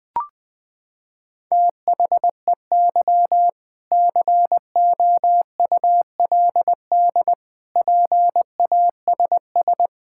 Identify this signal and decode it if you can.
Morse code, heard as a keyed tone, about 20 wpm, 700 Hz. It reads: THEY COULD PASS
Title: Sentences from Top 1000 Words - Review Part 6 - 20wpm